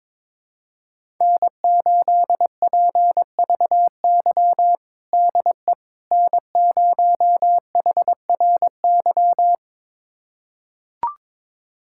Morse code, heard as a keyed tone, about 22 wpm, 700 Hz. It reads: N8PVY DE N0HRY
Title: USA Callsign DE Callsign - 5 Characters - 22wpm